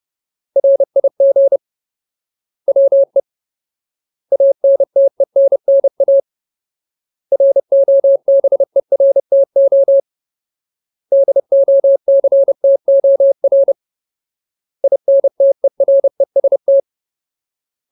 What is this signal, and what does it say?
Morse code, heard as a keyed tone, about 30 wpm, 550 Hz. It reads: RIG WE ANTENNA ROBERTO DOCTOR INTEREST
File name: Ultimate Word List / 30WPM Ultimate